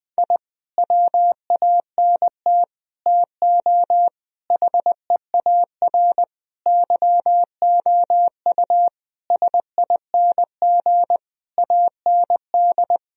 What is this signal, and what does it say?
Morse code, heard as a keyed tone, about 20 wpm, 700 Hz. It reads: I WANT TO HEAR YOU SING AND
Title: Sentences from Top 1000 Words - Review Part 4 - 20wpm